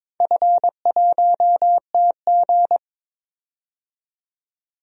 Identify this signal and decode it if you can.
Morse code, heard as a keyed tone, about 22 wpm, 700 Hz. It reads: F1TG
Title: DXCC-100 Call Signs - 4 Characters - 22wpm